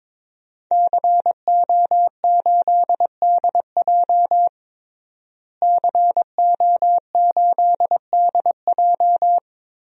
Morse code, heard as a keyed tone, about 22 wpm, 700 Hz. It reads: CO8DJ CO8DJ
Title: DXCC-100 Call Signs - 5 Characters - Repeated 3x - 22wpm